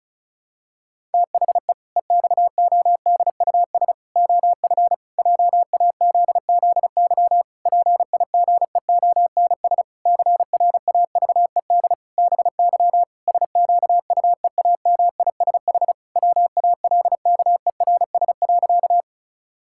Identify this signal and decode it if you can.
Morse code, heard as a keyed tone, about 36 wpm, 700 Hz. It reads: THE EXODUS OF JAZZY PIGEONS CRAVED BY SQUEAMISH WALKERS.
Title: Pangram Challenge v2 - 35wpm